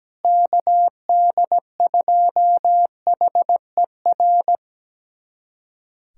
Morse code, heard as a keyed tone, about 17 wpm, 700 Hz. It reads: KD2HER
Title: USA Callsign DE Callsign - 6 Characters - 17wpm